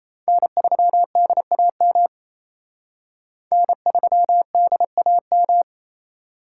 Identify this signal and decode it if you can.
Morse code, heard as a keyed tone, about 35 wpm, 700 Hz. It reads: N3DAM N3DAM
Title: Speed Racing - USA 1x3 Call Signs 28wpm